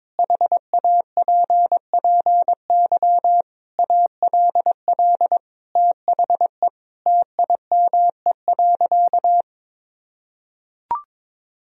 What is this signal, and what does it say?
Morse code, heard as a keyed tone, about 22 wpm, 700 Hz. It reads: HAPPY ALL THE TIME.
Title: Sentences from Top 1000 Words - Review Part 4 - 22wpm